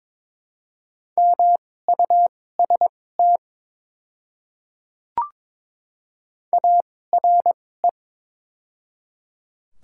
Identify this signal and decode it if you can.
Morse code, heard as a keyed tone, about 22 wpm, spaced as for 16 wpm, 700 Hz. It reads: MUST ARE
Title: 2x Character Spacing - Top 200 Words - 22wpm